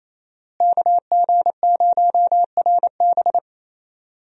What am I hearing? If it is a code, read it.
Morse code, heard as a keyed tone, about 28 wpm, 700 Hz. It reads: KG0RB
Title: USA Call Signs - 5 Characters - Repeated 3x - 28wpm